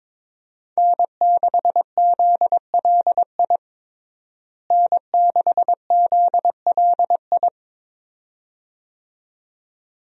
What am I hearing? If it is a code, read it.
Morse code, heard as a keyed tone, about 22 wpm, 700 Hz. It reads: N6ZLI N6ZLI
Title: USA Call Signs - 5 Characters - Repeated 2x - 22wpm